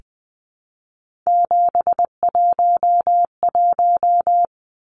Morse code, heard as a keyed tone, about 20 wpm, 700 Hz. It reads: 711
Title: Numbers - 3-Digit 20wpm